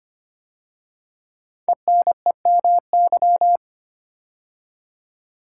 Morse code, heard as a keyed tone, about 25 wpm, 700 Hz. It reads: ENEMY
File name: 5 Letter Words - First 3 Letters Given - 25wpm